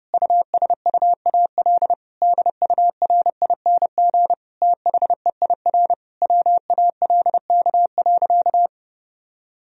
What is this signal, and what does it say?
Morse code, heard as a keyed tone, about 30 wpm, 700 Hz. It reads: USUAL DURING THEIR WALK.
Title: Sentences from Top 1000 Words - Review Part 2 - 30wpm